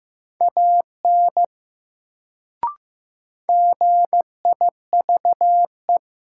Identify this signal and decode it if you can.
Morse code, heard as a keyed tone, about 15 wpm, 700 Hz. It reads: AN GIVE
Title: Sets of 2 Words - New Word Drill - 15wpm